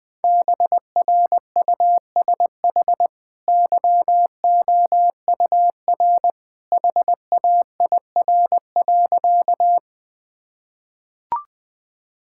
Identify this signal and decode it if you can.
Morse code, heard as a keyed tone, about 20 wpm, 700 Hz. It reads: BRUSH YOUR HAIR.
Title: Easy Sentences with Hints - 20wpm